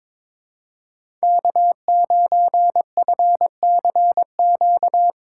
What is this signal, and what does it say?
Morse code, heard as a keyed tone, about 22 wpm, 700 Hz. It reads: K9FCQ
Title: USA Call Signs - 3x Character Spacing - 22wpm